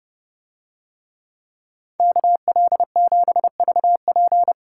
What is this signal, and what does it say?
Morse code, heard as a keyed tone, about 29 wpm, 700 Hz. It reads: KL7VP